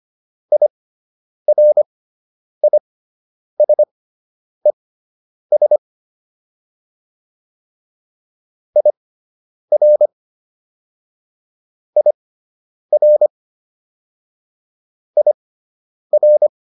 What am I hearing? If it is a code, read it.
Morse code, heard as a keyed tone, about 25 wpm, spaced as for 9 wpm, 600 Hz. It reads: IRISES IR IR IR